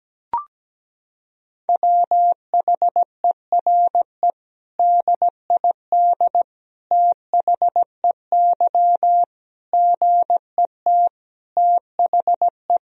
Morse code, heard as a keyed tone, about 17 wpm, 700 Hz. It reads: WHERE DID THEY GET THE
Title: Sentences from Top 400 Words and Abbreviations 17wpm